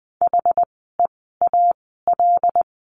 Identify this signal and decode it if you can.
Morse code, heard as a keyed tone, about 20 wpm, spaced as for 14 wpm, 700 Hz. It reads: HEAL
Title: 075 Lesson 13f - Letter L - Words 20wpm